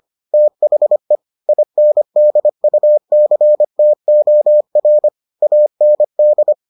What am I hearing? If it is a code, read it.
Morse code, heard as a keyed tone, about 25 wpm, 600 Hz. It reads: THE INDUCTOR AND